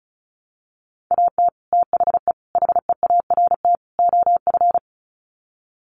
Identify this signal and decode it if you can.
Morse code, heard as a keyed tone, about 34 wpm, 700 Hz. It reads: AT THE HEART OF
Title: Sets of 4 Words 35wpm